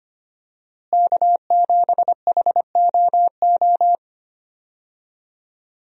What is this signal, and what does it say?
Morse code, heard as a keyed tone, about 25 wpm, 700 Hz. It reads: K7HOO
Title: USA Call Signs - 5 Characters - Repeated 3x - 25wpm